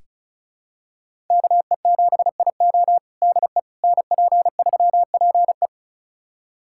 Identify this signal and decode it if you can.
Morse code, heard as a keyed tone, about 36 wpm, 700 Hz. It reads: KE7IO DE NP3PE
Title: USA Callsign DE Callsign - 5 Characters - 35wpm